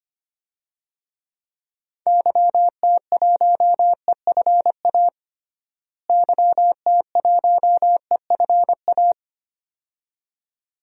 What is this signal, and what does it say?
Morse code, heard as a keyed tone, about 25 wpm, 700 Hz. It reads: YT1EFA YT1EFA
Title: DXCC-100 Call Signs - 6 Characters - Repeated 2x - 25wpm